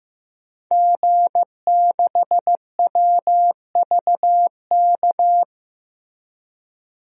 Morse code, heard as a keyed tone, about 16 wpm, 700 Hz. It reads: G6WVK
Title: DXCC-100 Call Signs - 5 Characters - Repeated 3x - 15wpm